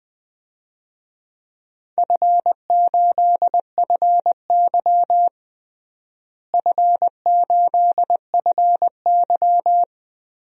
Morse code, heard as a keyed tone, about 20 wpm, 700 Hz. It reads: F8FY F8FY
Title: DXCC-100 Call Signs - 4 Characters - Repeated 2x - 20wpm